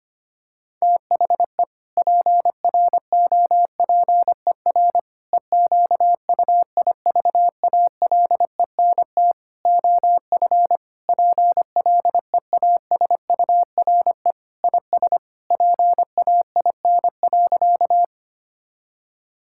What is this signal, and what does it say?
Morse code, heard as a keyed tone, about 25 wpm, 700 Hz. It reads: THE PROPER EQUIVALENT OF PLEASURE IS PAIN.
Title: Sentences from Top 1600 Words 25wpm